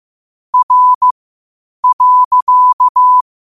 Morse code, heard as a keyed tone, about 15 wpm, spaced as for 8 wpm, 1000 Hz. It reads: R.
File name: qotdfiles / 8wpm